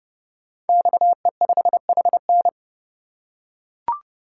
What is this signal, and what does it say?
Morse code, heard as a keyed tone, about 30 wpm, 700 Hz. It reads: XE5HN
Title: Code Groups - 5 Letter:Numbers - 30wpm